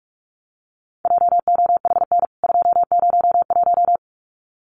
Morse code, heard as a keyed tone, about 44 wpm, 700 Hz. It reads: JOHN 201
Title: CWT Exchange 45wpm